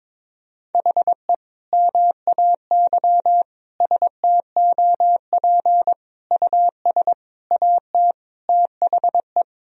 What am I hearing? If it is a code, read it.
Morse code, heard as a keyed tone, about 21 wpm, 700 Hz. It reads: HE MAY STOP US AT THE
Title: Sentences from Top 1000 Words - Review Part 2 - 22wpm